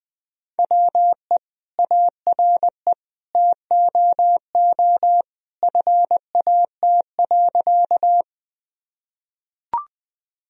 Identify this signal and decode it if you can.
Morse code, heard as a keyed tone, about 20 wpm, 700 Hz. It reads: WE ARE TOO FAT.
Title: Sentences from Top 1000 Words - Review Part 4 - 20wpm